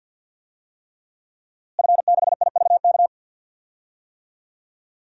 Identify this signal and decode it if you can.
Morse code, heard as a keyed tone, about 52 wpm, 700 Hz. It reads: F6IVK
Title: DXCC-100 Call Signs - 5 Characters - Repeated 3x - 50wpm